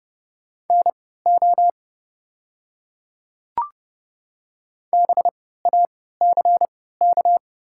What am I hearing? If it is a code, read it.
Morse code, heard as a keyed tone, about 30 wpm, spaced as for 17 wpm, 700 Hz. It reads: NO BACK
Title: 3x Character Spacing - Top 300 Words - 30wpm